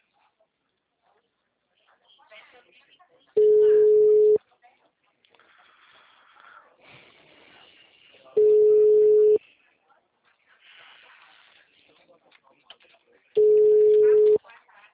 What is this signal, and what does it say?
Ringback tone (Europe, 425 Hz), with speech.